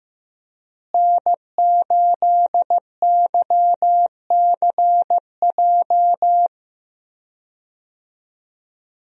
Morse code, heard as a keyed tone, about 15 wpm, 700 Hz. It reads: N8YCJ
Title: USA Call Signs - 5 Characters - Repeated 3x - 15wpm